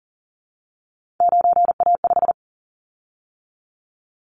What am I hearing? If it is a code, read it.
Morse code, heard as a keyed tone, about 42 wpm, 690 Hz. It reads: 9A5
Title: Callsign Prefixes for Top 100 DXCC - 40wpm